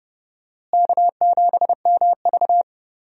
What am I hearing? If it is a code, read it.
Morse code, heard as a keyed tone, about 31 wpm, 700 Hz. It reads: K7MV
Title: USA Call Signs - 4 Characters - Repeated 2x - 30wpm